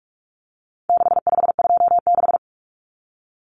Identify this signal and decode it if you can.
Morse code, heard as a keyed tone, about 44 wpm, 700 Hz. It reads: 6526
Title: Numbers - 4-Digit 45wpm